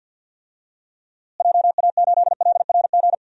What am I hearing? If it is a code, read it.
Morse code, heard as a keyed tone, about 47 wpm, 700 Hz. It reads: JA8LRG
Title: DXCC-100 Call Signs - 6 Characters - Repeated 3x - 50wpm